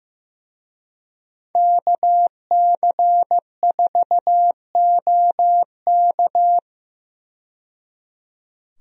Morse code, heard as a keyed tone, about 15 wpm, 700 Hz. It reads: KC4OK
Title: USA Call Signs - 5 Characters - 15wpm